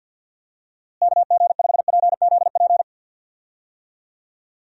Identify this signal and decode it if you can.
Morse code, heard as a keyed tone, about 48 wpm, 700 Hz. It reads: KG5PZP
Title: USA Call Signs - 6 Characters - Repeated 3x - 50wpm